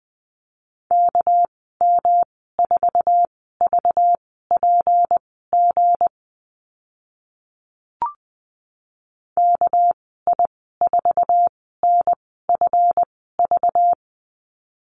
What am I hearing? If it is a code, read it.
Morse code, heard as a keyed tone, about 20 wpm, spaced as for 15 wpm, 700 Hz. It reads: KM4VPG KI4NFV
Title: 200 Lesson 32g - Letter k - Call Signs 20wpm